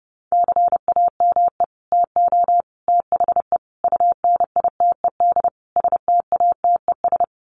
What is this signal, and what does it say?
Morse code, heard as a keyed tone, about 30 wpm, 700 Hz. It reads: CAME TO THE UNITED STATES
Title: Sets of 5 Words Encore2 30wpm